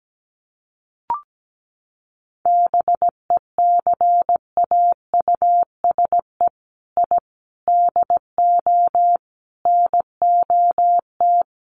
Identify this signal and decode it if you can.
Morse code, heard as a keyed tone, about 17 wpm, 700 Hz. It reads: BECAUSE I DO NOT
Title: Sets of 5 Words Encore2 17wpm